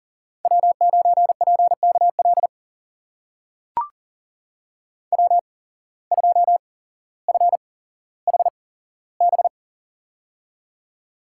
Morse code, heard as a keyed tone, about 40 wpm, spaced as for 17 wpm, 700 Hz. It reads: W9PKL W2FHB
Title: USA Call Signs - 5 Characters - 8x Character Spacing - 40wpm